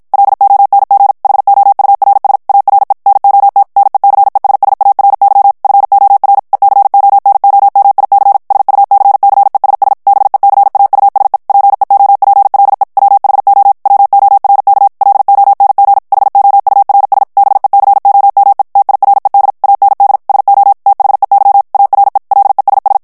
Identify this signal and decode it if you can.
Morse code, heard as a keyed tone, about 56 wpm, 800 Hz. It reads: LONG HOURS ARE NOT NECESSARY FOR ECONOMIC SUCCESS BECAUSE PEOPLE WHO WORK LONG HOURS BECOME TIRED AND SO THEY ARE LESS
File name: morse mp3 / ANGLAIS texte clair 1 vitesse 50 mots minute